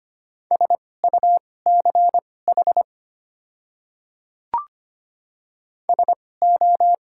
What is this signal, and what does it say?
Morse code, heard as a keyed tone, about 25 wpm, spaced as for 18 wpm, 700 Hz. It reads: SUCH SO
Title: 2x Character Spacing - Top 200 Words - 25wpm